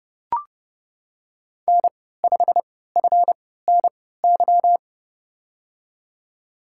Morse code, heard as a keyed tone, about 30 wpm, spaced as for 17 wpm, 700 Hz. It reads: N5FNY
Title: USA Call Signs - 5 Characters - 3x Character Spacing - 30wpm